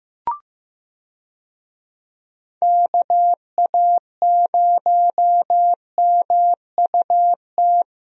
Morse code, heard as a keyed tone, about 15 wpm, 700 Hz. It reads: KA0MUT
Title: USA Call Signs - 6 Characters - Repeated 3x - 15wpm